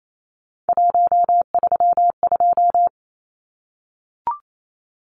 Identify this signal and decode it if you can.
Morse code, heard as a keyed tone, about 28 wpm, 700 Hz. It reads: 132
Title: Numbers - 3-Digit 28wpm